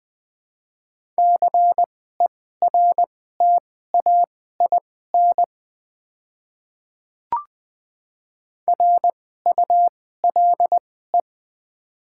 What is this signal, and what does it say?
Morse code, heard as a keyed tone, about 20 wpm, spaced as for 14 wpm, 700 Hz. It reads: CERTAIN RULE
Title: 2x Character Spacing - Top 500 Words - 20wpm